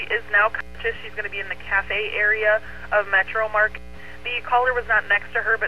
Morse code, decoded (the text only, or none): none